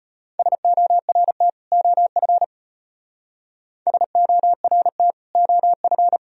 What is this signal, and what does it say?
Morse code, heard as a keyed tone, about 32 wpm, 700 Hz. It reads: SORT OF SORT OF